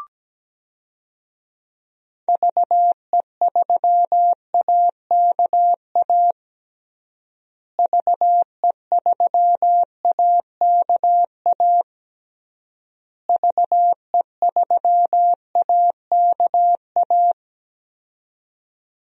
Morse code, heard as a keyed tone, about 17 wpm, 700 Hz. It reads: VE3AKA VE3AKA VE3AKA
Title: DXCC-100 Call Signs - 6 Characters - Repeated 3x - 17wpm